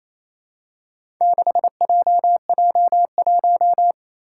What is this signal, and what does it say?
Morse code, heard as a keyed tone, about 28 wpm, 700 Hz. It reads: 6JJ1
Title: Four Challenging Letter-Number Combinations 28wpm